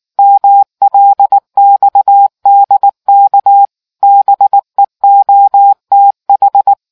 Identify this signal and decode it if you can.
Morse code, heard as a keyed tone, about 19 wpm, 800 Hz. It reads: MLXDK BEOTH